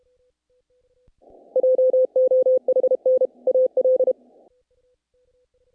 Morse code, heard as a keyed tone, about 28 wpm, 510 Hz. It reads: JOHN AL
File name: contests / naqp-cw-k5tr-op-k5pi- -1800